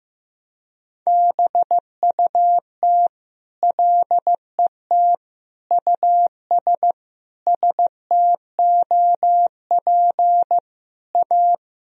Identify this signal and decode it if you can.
Morse code, heard as a keyed tone, about 15 wpm, 700 Hz. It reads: BUT LET US STOP A
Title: Sentences from Top 1000 Words - Review Part 1 - 15wpm